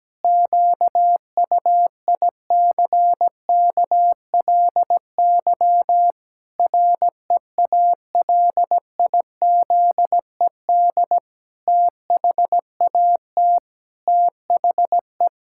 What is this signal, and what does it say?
Morse code, heard as a keyed tone, about 17 wpm, 700 Hz. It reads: QUICKLY REALIZED THAT THE
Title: Pangram Challenge v2 - 17wpm